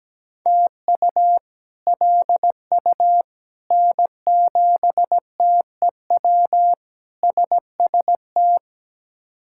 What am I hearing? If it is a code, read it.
Morse code, heard as a keyed tone, about 17 wpm, 700 Hz. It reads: TU LU N7TEW SST
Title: K1USN SST QSO Practice - 17wpm